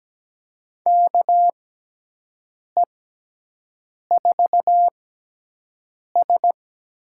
Morse code, heard as a keyed tone, about 17 wpm, spaced as for 6 wpm, 700 Hz. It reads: KE4S
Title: USA Call Signs - 6 Characters - 6x Character Spacing - 17wpm